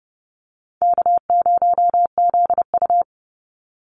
Morse code, heard as a keyed tone, about 30 wpm, 700 Hz. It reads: K0ZU
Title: Easy Call Signs - 30wpm